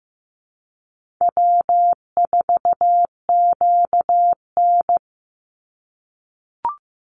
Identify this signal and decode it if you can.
Morse code, heard as a keyed tone, about 15 wpm, 700 Hz. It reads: W4QN
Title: Easy Call Signs - 15wpm